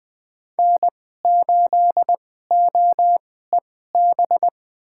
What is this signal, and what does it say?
Morse code, heard as a keyed tone, about 20 wpm, spaced as for 14 wpm, 700 Hz. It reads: N8OEB
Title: USA Call Signs - 5 Characters - 2x Character Spacing - 20wpm